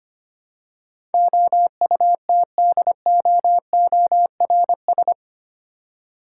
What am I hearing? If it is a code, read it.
Morse code, heard as a keyed tone, about 25 wpm, 700 Hz. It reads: OUTDOORS
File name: 8 Letter Words - First 3 Letters Given - 25wpm